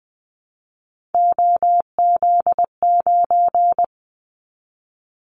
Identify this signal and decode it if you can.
Morse code, heard as a keyed tone, about 19 wpm, 700 Hz. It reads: OZ9
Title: Callsign Prefixes for Top 50 DXCC - 20wpm